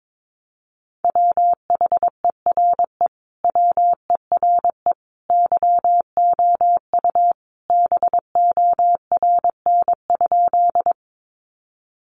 Morse code, heard as a keyed tone, about 22 wpm, 700 Hz. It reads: WHERE WERE YOU BORN?
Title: Sentences from Top 1000 Words 22wpm